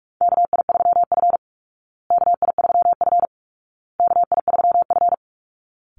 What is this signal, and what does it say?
Morse code, heard as a keyed tone, about 44 wpm, 700 Hz. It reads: KI3F KI3F KI3F
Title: Call Signs Repeated 2x - 45wpm